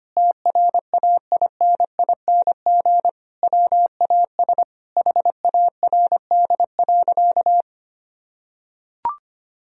Morse code, heard as a keyed tone, about 25 wpm, 700 Hz. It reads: TRAINING WAS HARD.
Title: Sentences from Top 1000 Words - Review Part 4 - 25wpm